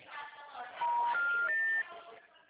Special information tone, with noise.